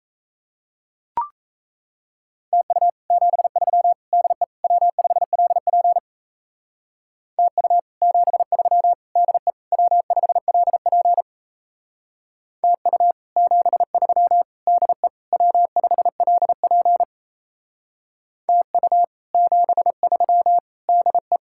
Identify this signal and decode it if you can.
Morse code, heard as a keyed tone, about 36 wpm, 700 Hz. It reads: TU 73 DE W5LP TU 73 DE W5LP TU 73 DE W5LP TU 73 DE